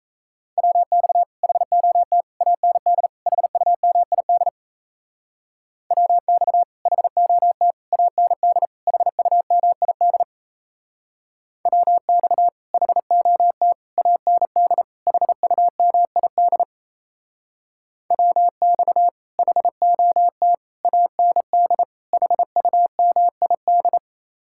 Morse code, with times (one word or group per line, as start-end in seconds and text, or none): 0.6-1.2 WX
1.4-2.2 HOT
2.4-3.1 AND
3.3-4.5 HUMID
5.9-6.6 WX
6.9-7.7 HOT
7.9-8.7 AND
8.9-10.2 HUMID
11.7-12.5 WX
12.7-13.7 HOT
14.0-14.8 AND
15.1-16.6 HUMID
18.1-19.1 WX
19.4-20.5 HOT
20.8-21.8 AND
22.1-24.0 HUMID